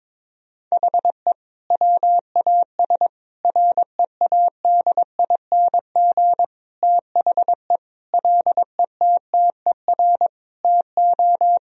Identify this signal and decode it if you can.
Morse code, heard as a keyed tone, about 22 wpm, 700 Hz. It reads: HE WAS READING THE LETTER TO
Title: Sentences from Top 1000 Words - Review Part 3 - 22wpm